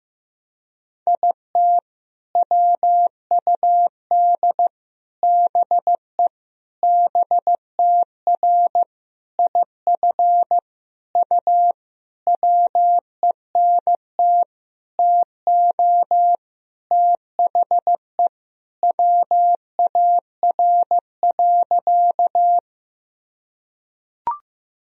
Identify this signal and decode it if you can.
Morse code, heard as a keyed tone, about 15 wpm, 700 Hz. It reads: IT WUD BE BTR IF U WENT TO THE WAR.
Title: Sentences from Top 400 Words and Abbreviations 15wpm